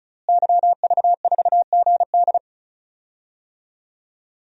Morse code, heard as a keyed tone, about 34 wpm, 700 Hz. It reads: YV4GD
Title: DXCC-100 Call Signs - 5 Characters - Repeated 2x - 35wpm